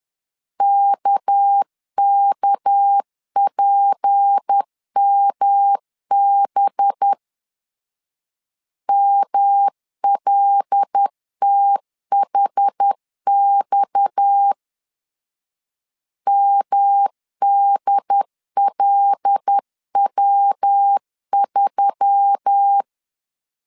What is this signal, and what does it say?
Morse code, heard as a keyed tone, about 10 wpm, 800 Hz. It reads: KKPMB MLTHX MDLW3
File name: CW / DET / 1 6 1500C S B 10wpm 4137